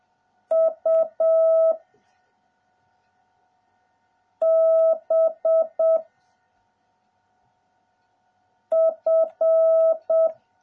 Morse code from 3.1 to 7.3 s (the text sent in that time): B